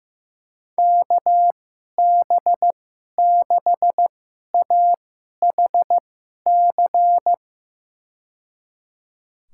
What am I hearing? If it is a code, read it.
Morse code, heard as a keyed tone, about 15 wpm, spaced as for 11 wpm, 700 Hz. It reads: KB6AHC
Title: USA Call Signs - 6 Characters - 2x Character Spacing - 15wpm